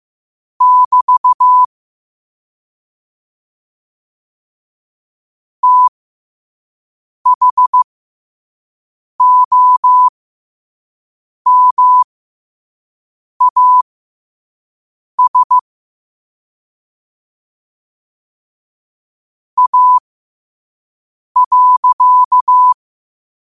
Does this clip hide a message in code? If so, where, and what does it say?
Yes: Morse code from 0.6 to 22.7 s, heard as a keyed tone, about 15 wpm, spaced as for 5 wpm, 1000 Hz. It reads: <BT> THOMAS A.